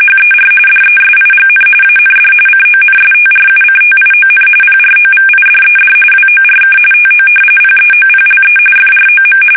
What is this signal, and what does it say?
RTTY, about 45.45 baud. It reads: HER DESIGN AND CONSTRUCTION PROVIDED WORK TO PEOPLE FROM
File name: fsk850 / battleshipnj